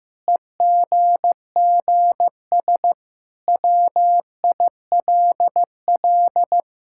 Morse code, heard as a keyed tone, about 16 wpm, 700 Hz. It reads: EGGS WILL
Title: Pangram Challenge v2 - 15wpm